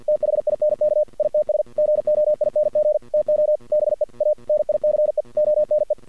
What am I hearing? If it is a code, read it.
Morse code, heard as a keyed tone, about 50 wpm, 610 Hz. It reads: THINK ITS GOING TO BE A NICE ONE
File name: archived / CRSnet